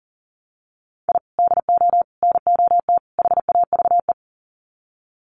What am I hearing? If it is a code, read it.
Morse code, heard as a keyed tone, about 39 wpm, 700 Hz. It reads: I DO NOT HAVE
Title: Sets of 4 Words 40wpm